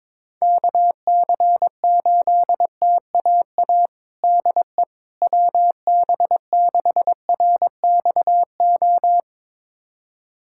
Morse code, heard as a keyed tone, about 22 wpm, 700 Hz. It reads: KC8TAA DE WB6RXO
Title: USA Callsign DE Callsign - 6 Characters - 22wpm